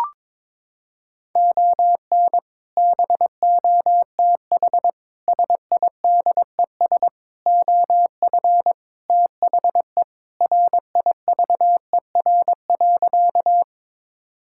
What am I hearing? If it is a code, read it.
Morse code, heard as a keyed tone, about 21 wpm, 700 Hz. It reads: ON BOTH SIDES OF THE RIVER.
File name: practice / 22wpm / Sentences with 4-Word Phrases - 22wpm